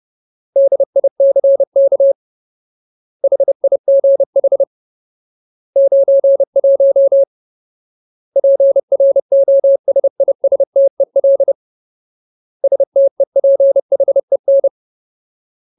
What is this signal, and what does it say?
Morse code, heard as a keyed tone, about 30 wpm, 550 Hz. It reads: DICK HIGH 91 PROSISTEL STEPHEN
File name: Ultimate Word List / 30WPM Ultimate